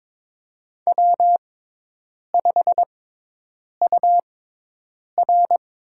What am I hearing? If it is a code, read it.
Morse code, heard as a keyed tone, about 21 wpm, spaced as for 8 wpm, 700 Hz. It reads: W5UR
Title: USA Call Signs - 5 Characters - 6x Character Spacing - 22wpm